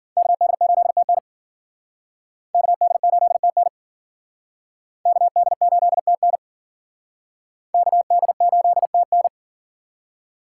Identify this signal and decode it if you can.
Morse code, heard as a keyed tone, about 50 wpm, 700 Hz. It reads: KD8TN KD8TN KD8TN KD8TN